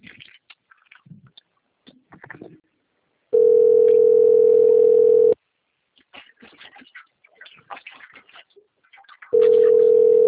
Ringback tone (North America), 440 + 480 Hz.